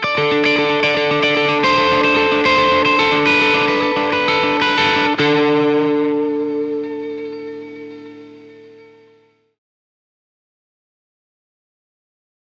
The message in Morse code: none